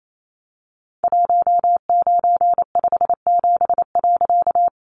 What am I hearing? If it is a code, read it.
Morse code, heard as a keyed tone, about 28 wpm, 700 Hz. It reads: 1957.
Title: A Sound Of Thunder 28wpm